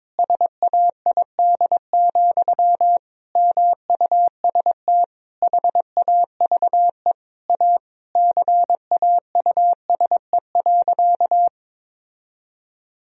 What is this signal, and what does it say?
Morse code, heard as a keyed tone, about 22 wpm, 700 Hz. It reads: SAID, MUST HAVE A CAUSE.
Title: Sentences from Top 1000 Words - Review Part 1 - 22wpm